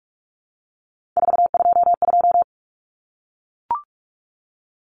Morse code, heard as a keyed tone, about 44 wpm, 700 Hz. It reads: 422
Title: Common Cut Numbers - 3-Digit 45wpm